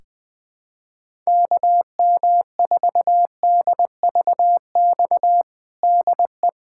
Morse code, heard as a keyed tone, about 21 wpm, 700 Hz. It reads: KM4DVX DE
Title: USA Callsign DE Callsign 20wpm